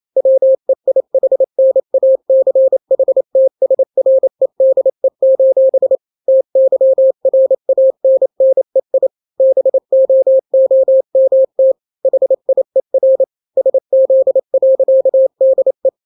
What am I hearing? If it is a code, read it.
Morse code, heard as a keyed tone, about 28 wpm, 530 Hz. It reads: WEIHNACHTSREDE: TYRANNEI BOOMT HIER SZ.DE
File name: daily / 27wpm